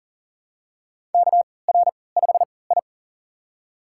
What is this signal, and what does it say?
Morse code, heard as a keyed tone, about 40 wpm, spaced as for 23 wpm, 700 Hz. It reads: KR5I
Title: USA Call Signs - 4 Characters - 3x Character Spacing - 40wpm